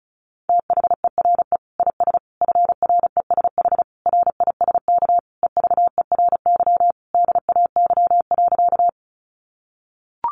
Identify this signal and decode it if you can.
Morse code, heard as a keyed tone, about 34 wpm, 700 Hz. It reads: THERE IS FRESH RISK EVERY DAY.
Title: Sentences from Top 1200 Words 35wpm